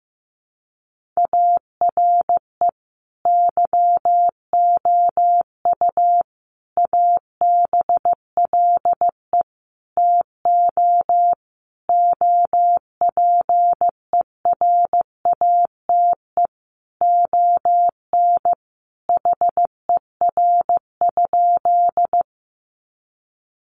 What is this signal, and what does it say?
Morse code, heard as a keyed tone, about 15 wpm, 700 Hz. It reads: ARE YOU ABLE TO OPERATE ON HER?
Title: Sentences from Top 1500 Words 15wpm